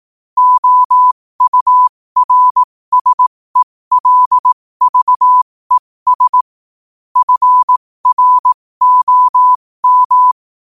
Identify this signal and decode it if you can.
Morse code, heard as a keyed tone, about 18 wpm, spaced as for 15 wpm, 1000 Hz. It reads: OURSELVES FROM
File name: qotdfiles / 15wpm